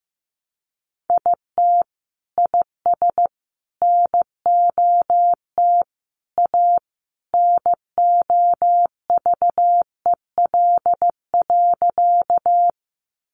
Morse code, heard as a keyed tone, about 15 wpm, 700 Hz. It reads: IT IS NOT A NOVEL.
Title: Sentences from Top 1300 Words 15wpm